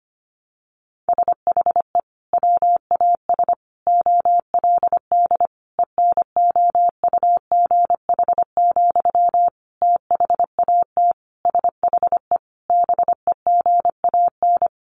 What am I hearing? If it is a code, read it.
Morse code, heard as a keyed tone, about 25 wpm, 700 Hz. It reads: SHE WAS OLD ENOUGH, THAT SHE BEGAN
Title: Sentences from Top 800 Words 25wpm